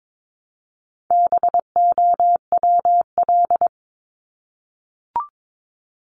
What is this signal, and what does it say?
Morse code, heard as a keyed tone, about 22 wpm, 700 Hz. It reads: BOWL
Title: Top 1001-1250 Words 22wpm